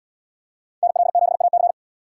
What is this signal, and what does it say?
Morse code, heard as a keyed tone, about 70 wpm, 700 Hz. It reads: NL7AC